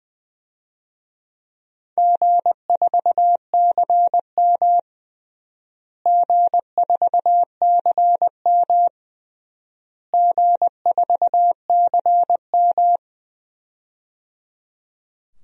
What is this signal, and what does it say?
Morse code, heard as a keyed tone, about 20 wpm, 700 Hz. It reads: G4CM G4CM G4CM
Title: DXCC-100 Call Signs - 4 Characters - Repeated 3x - 20wpm